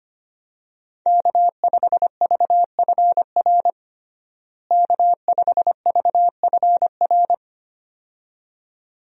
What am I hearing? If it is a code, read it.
Morse code, heard as a keyed tone, about 25 wpm, 700 Hz. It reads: K5VFR K5VFR
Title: USA Call Signs - 5 Characters - Repeated 2x - 25wpm